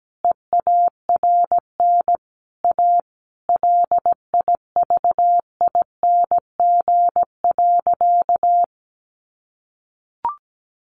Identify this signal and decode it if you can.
Morse code, heard as a keyed tone, about 17 wpm, 700 Hz. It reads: EARN A LIVING.
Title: Sentences from Top 1500 Words 17wpm